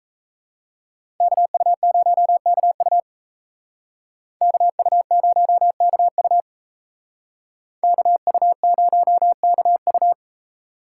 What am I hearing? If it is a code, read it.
Morse code, heard as a keyed tone, about 32 wpm, 700 Hz. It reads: KU0KU KU0KU KU0KU
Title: Speed Racing - USA 2x2 Call Signs 28wpm